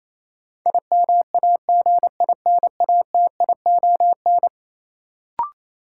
Morse code, heard as a keyed tone, about 28 wpm, 700 Hz. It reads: IMAGINATION
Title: Sentences from Top 1300 Words - New Word Drill - 28wpm